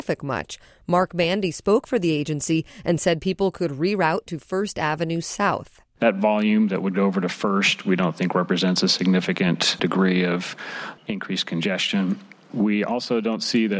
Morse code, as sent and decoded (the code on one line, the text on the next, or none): none